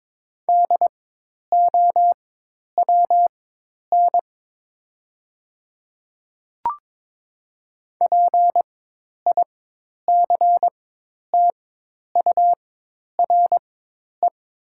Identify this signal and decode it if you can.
Morse code, heard as a keyed tone, about 22 wpm, spaced as for 10 wpm, 700 Hz. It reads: DOWN PICTURE
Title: 4x Character Spacing - Top 300 Words - 22wpm